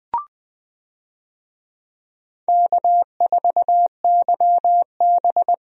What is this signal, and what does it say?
Morse code, heard as a keyed tone, about 21 wpm, 700 Hz. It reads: K4YB
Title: USA Call Signs - 4 Characters - 20wpm